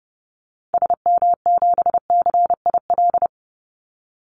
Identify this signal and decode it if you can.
Morse code, heard as a keyed tone, about 30 wpm, 700 Hz. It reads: SM7CIL
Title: Call Signs - 30wpm